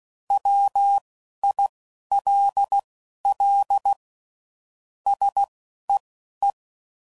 Morse code, heard as a keyed tone, about 16 wpm, spaced as for 11 wpm, 800 Hz. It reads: WILL SEE
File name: CW Code Groups @10WPM/Technical Subjects